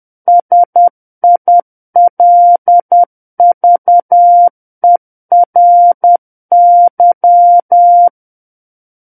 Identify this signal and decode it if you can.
Morse code, heard as a keyed tone, about 10 wpm, 700 Hz. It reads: SILVERY
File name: Book One / 01-Chapter 1 - 10 WPM